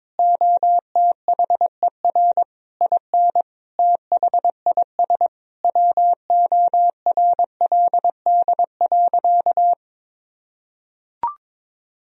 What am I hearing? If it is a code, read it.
Morse code, heard as a keyed tone, about 22 wpm, 700 Hz. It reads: OTHER IN THIS WORLD.
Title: Sentences from Top 1000 Words - Review Part 1 - 22wpm